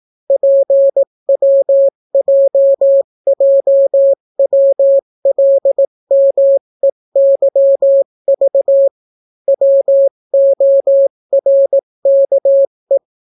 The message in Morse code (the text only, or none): PWJJWLMEYV WORKE